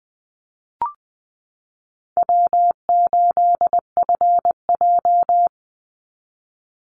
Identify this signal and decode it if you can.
Morse code, heard as a keyed tone, about 20 wpm, 700 Hz. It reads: W8FJ
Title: Easy Call Signs Repeated 2x - 20wpm